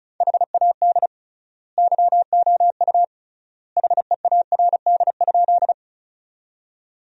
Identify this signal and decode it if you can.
Morse code, heard as a keyed tone, about 34 wpm, 700 Hz. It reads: HAD YOU HEARD?
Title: 3x Word-Spacing - Sentences from Top 400 Words - 35wpm